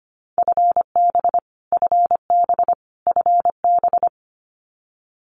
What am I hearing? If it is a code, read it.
Morse code, heard as a keyed tone, about 25 wpm, 700 Hz. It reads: FB FB FB
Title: Abbreviations Repeated 3x - 25wpm